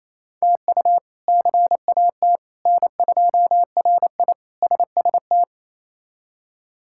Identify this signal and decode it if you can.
Morse code, heard as a keyed tone, about 28 wpm, 700 Hz. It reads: TU CAT N2RI SST
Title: K1USN SST QSO Practice - 28wpm